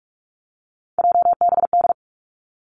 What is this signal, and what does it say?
Morse code, heard as a keyed tone, about 46 wpm, 700 Hz. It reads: JBD
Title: Three Challenging Letter-Number Combinations 45wpm